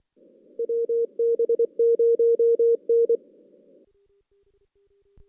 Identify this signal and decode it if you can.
Morse code, heard as a keyed tone, about 24 wpm, 460 Hz. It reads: WB0N
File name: 2016 JAN CW NAQP / 2016 Jan naqp k5ot - 0-6